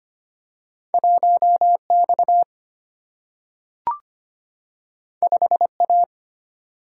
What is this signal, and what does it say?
Morse code, heard as a keyed tone, about 25 wpm, 700 Hz. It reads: 1X 5A
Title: Single-Letter Single-Number 25wpm